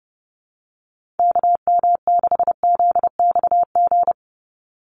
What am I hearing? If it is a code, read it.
Morse code, heard as a keyed tone, about 30 wpm, 700 Hz. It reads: KM6ZXG
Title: USA Call Signs 30wpm